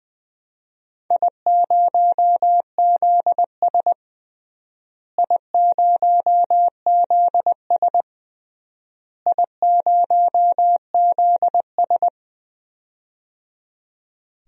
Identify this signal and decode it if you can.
Morse code, heard as a keyed tone, about 20 wpm, 700 Hz. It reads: I0ZS I0ZS I0ZS
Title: DXCC-100 Call Signs - 4 Characters - Repeated 3x - 20wpm